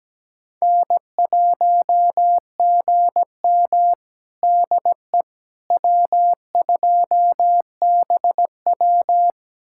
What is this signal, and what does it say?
Morse code, heard as a keyed tone, about 17 wpm, 700 Hz. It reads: N1GM DE W2BW
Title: USA Callsign DE Callsign - 4 Characters - 17wpm